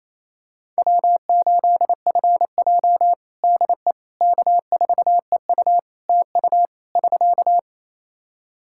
Morse code, heard as a keyed tone, about 27 wpm, 700 Hz. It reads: W8FJ DE K4EU TU <SK>
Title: Practice QSO - Exchange 4 - Ending - 28wpm